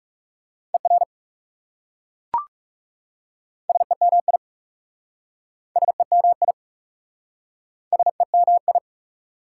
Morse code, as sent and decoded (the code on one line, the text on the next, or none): . .-. / ... -- .. / ... -- .. / ... -- ..
ER SEMI SEMI SEMI